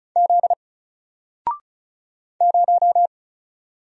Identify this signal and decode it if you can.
Morse code, heard as a keyed tone, about 36 wpm, 700 Hz. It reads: Z 0
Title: Single Letter-Number Alt 35wpm